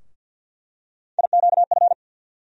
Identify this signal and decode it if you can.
Morse code, heard as a keyed tone, about 48 wpm, 700 Hz. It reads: IQF
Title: USA Call Sign Postfixes - 3 Letters 50wpm